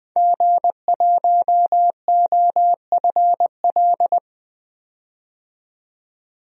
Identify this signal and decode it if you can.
Morse code, heard as a keyed tone, about 20 wpm, 700 Hz. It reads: G1OFL
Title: DXCC-100 Call Signs - 5 Characters - 20wpm